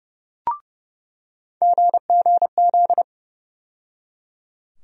Morse code, heard as a keyed tone, about 30 wpm, 700 Hz. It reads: GGZ
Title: USA Call Sign Postfixes - 3 Letters 30wpm